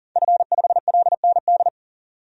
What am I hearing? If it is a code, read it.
Morse code, heard as a keyed tone, about 40 wpm, 700 Hz. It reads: F5LND
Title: DXCC-100 Call Signs - 5 Characters - 40wpm